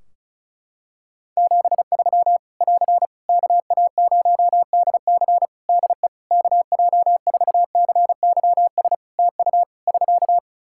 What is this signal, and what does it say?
Morse code, heard as a keyed tone, about 34 wpm, 700 Hz. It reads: 73 <AR> KA0DC DE KJ4CYS TU <SK>
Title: K1USN SST QSO Practice 35wpm